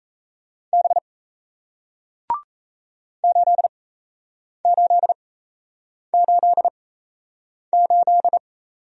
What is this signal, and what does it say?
Morse code, heard as a keyed tone, about 38 wpm, spaced as for 8 wpm, 700 Hz. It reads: B 8888